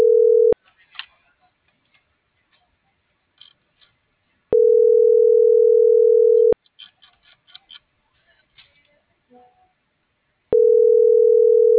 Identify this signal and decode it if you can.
Ringback tone (North America), 440 + 480 Hz.